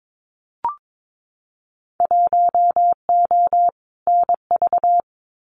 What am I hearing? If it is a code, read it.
Morse code, heard as a keyed tone, about 22 wpm, 700 Hz. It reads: 1O NV
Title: Winter Field Day Exchanges Encore 22wpm